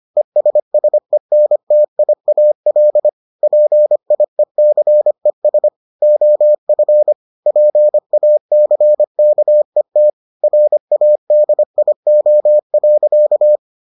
Morse code, heard as a keyed tone, about 25 wpm, 600 Hz. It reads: ESSENTIAL PIECES OF PACKET RADIO.